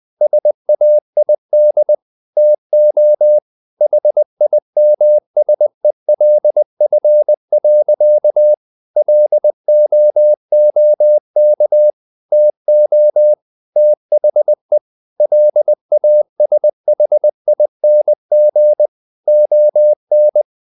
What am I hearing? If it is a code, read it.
Morse code, heard as a keyed tone, about 20 wpm, 600 Hz. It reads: SAID TO HIMSELF. LOOK TO THE LASHING ON
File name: the old man and the sea part 2 20wpm